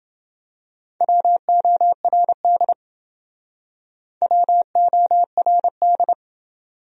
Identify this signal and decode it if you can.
Morse code, heard as a keyed tone, about 28 wpm, 700 Hz. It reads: WORD WORD